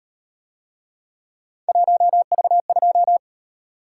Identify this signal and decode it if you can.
Morse code, heard as a keyed tone, about 37 wpm, 700 Hz. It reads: 1V2